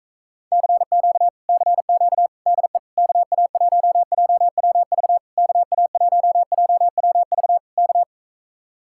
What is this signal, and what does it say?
Morse code, heard as a keyed tone, about 43 wpm, 700 Hz. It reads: CQ CQ DE KA1JWV KA1JWV K